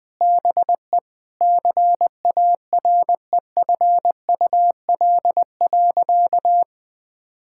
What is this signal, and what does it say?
Morse code, heard as a keyed tone, about 20 wpm, 700 Hz. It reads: BE CAREFUL.
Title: Sentences with 5-Word Phrases - 20wpm